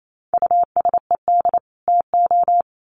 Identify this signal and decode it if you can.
Morse code, heard as a keyed tone, about 28 wpm, 700 Hz. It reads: USED TO
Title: Sets of 2 Words 28wpm